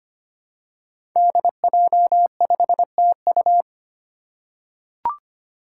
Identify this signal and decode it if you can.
Morse code, heard as a keyed tone, about 25 wpm, 700 Hz. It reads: DJ5TU
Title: Five Challenging Letter-Number Combinations 25wpm